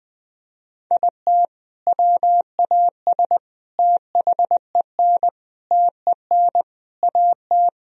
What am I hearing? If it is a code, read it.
Morse code, heard as a keyed tone, about 20 wpm, 700 Hz. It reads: IT WAS THEN TEN AT
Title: Sentences from Top 1000 Words - Review Part 2 - 20wpm